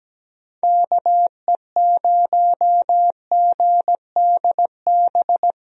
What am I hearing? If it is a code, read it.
Morse code, heard as a keyed tone, about 17 wpm, 700 Hz. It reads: KE0GDB
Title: USA Call Signs - 6 Characters - 17wpm